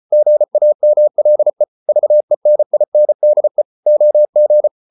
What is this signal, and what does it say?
Morse code, heard as a keyed tone, about 34 wpm, 600 Hz. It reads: GAMLE VENINDE OG
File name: Ferie QRQ 34wpm